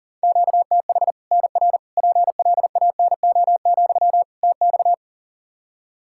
Morse code, heard as a keyed tone, about 40 wpm, 700 Hz. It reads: QTH NR PLANO, TX
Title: Practice QSO - Exchange 1 - Beginning - 40wpm